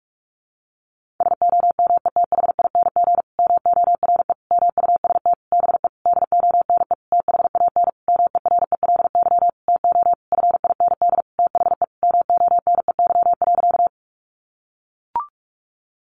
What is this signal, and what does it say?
Morse code, heard as a keyed tone, about 44 wpm, 700 Hz. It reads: SOMETHING MORE MUST BE DONE THAN MERELY TO FIND THE MONEY.
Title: Sentences from Top 1400 Words 45wpm